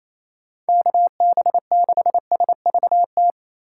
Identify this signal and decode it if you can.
Morse code, heard as a keyed tone, about 28 wpm, 700 Hz. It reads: KB6SVT
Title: USA Call Signs - 6 Characters - 4x Character Spacing - 28wpm